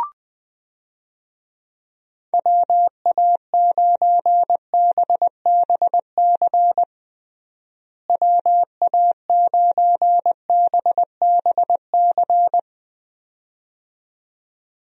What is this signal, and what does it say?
Morse code, heard as a keyed tone, about 21 wpm, 700 Hz. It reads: WA9BBC WA9BBC
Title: USA Call Signs - 6 Characters - Repeated 2x - 20wpm